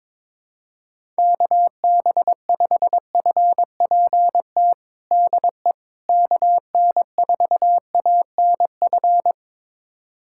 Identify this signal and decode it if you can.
Morse code, heard as a keyed tone, about 22 wpm, 700 Hz. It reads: KB5FPT DE KN4ANF
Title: USA Callsign DE Callsign - 6 Characters - 22wpm